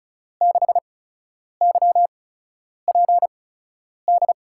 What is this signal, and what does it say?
Morse code, heard as a keyed tone, about 36 wpm, spaced as for 10 wpm, 700 Hz. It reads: 6YPD
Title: USA Call Signs - 6 Characters - 8x Character Spacing - 35wpm